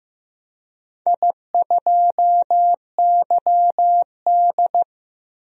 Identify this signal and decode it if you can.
Morse code, heard as a keyed tone, about 15 wpm, 700 Hz. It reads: I2YD
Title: DXCC-100 Call Signs - 4 Characters - 15wpm